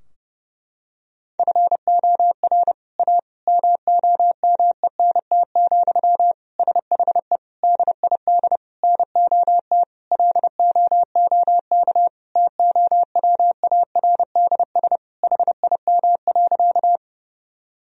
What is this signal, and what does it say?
Morse code, heard as a keyed tone, about 31 wpm, 700 Hz. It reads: FOR A MOMENT, SHE DID NOT LOOK TOWARDS HIM.
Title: Sentences from Top 1000 Words - Review Part 2 - 30wpm